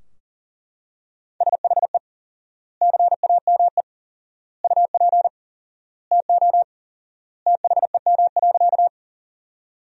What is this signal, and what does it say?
Morse code, heard as a keyed tone, about 37 wpm, 700 Hz. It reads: SHE CAME UP TO THEM.
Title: 4x Word-Spacing - Sentences from Top 200 Words - 40wpm